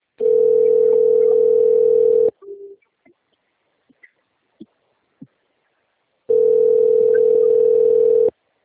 Ringback tone (North America), 440 + 480 Hz.